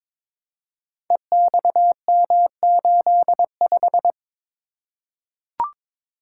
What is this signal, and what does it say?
Morse code, heard as a keyed tone, about 22 wpm, 700 Hz. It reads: EXM85
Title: Code Groups - 5 Letter:Numbers - 22wpm